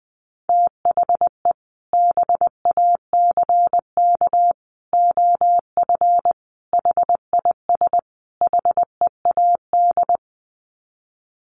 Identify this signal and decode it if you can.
Morse code, heard as a keyed tone, about 20 wpm, 700 Hz. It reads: THE BACK OF HIS HEAD
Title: Sets of 5 Words Encore 20wpm